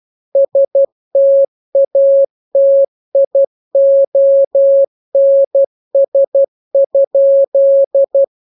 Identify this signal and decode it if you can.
Morse code, heard as a keyed tone, about 12 wpm, 550 Hz. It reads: STATIONS?